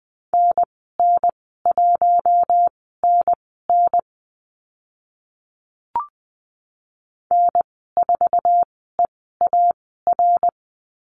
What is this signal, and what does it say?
Morse code, heard as a keyed tone, about 20 wpm, spaced as for 14 wpm, 700 Hz. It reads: NN1NN N4EAR
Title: 076 Lesson 13g - Letter L - Call Signs 20wpm